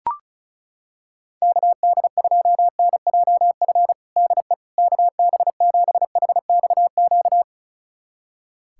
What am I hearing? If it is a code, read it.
Morse code, heard as a keyed tone, about 36 wpm, 700 Hz. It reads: KD2NJF DE KB7HXQ
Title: USA Callsign DE Callsign - 6 Characters - 35wpm